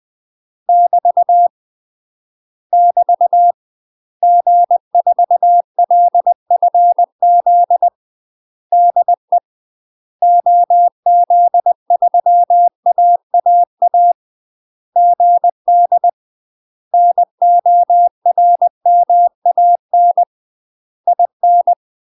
Morse code, heard as a keyed tone, about 20 wpm, 700 Hz. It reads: <BT> <BT> G4LFZ DE OZ3AAA GD NORMAN IN
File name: UK MASTER QSOs TRAINING/20WPM